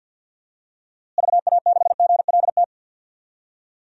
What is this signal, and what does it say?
Morse code, heard as a keyed tone, about 51 wpm, 700 Hz. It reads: VA6GLT
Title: DXCC-100 Call Signs - 6 Characters - Repeated 2x - 50wpm